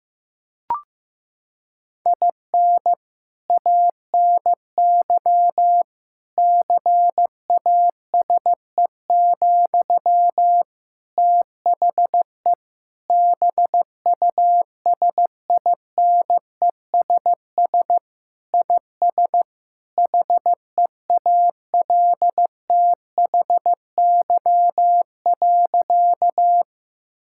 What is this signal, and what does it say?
Morse code, heard as a keyed tone, about 15 wpm, 700 Hz. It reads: IN ANY CASE, THE BUSINESS IS HEALTHY.
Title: Sentences from Top 1000 Words - Review Part 5 - 15wpm